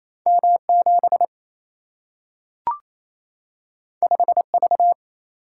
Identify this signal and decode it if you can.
Morse code, heard as a keyed tone, about 28 wpm, 700 Hz. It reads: M 7 5 V
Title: Single-Letter Single-Number 28wpm